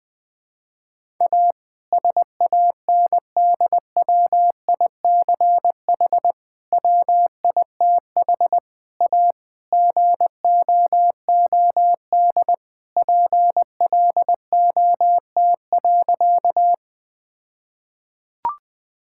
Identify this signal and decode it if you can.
Morse code, heard as a keyed tone, about 21 wpm, 700 Hz. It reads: A SANDWICH WITH A GOOD PLOT.
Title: All-Age Appropriate Jokes - 20wpm